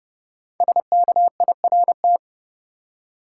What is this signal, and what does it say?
Morse code, heard as a keyed tone, about 32 wpm, 700 Hz. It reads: SKIRT
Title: Five Letter Words - 30wpm